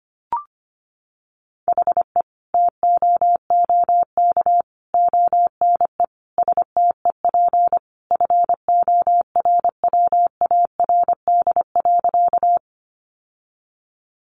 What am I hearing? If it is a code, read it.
Morse code, heard as a keyed tone, about 25 wpm, 700 Hz. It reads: HE TOOK ONE STEP FORWARD.
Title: Sentences from Top 1300 Words 25wpm